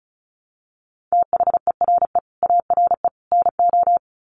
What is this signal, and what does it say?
Morse code, heard as a keyed tone, about 34 wpm, 700 Hz. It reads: THERE ARE NO
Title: Sets of 3 Words 35wpm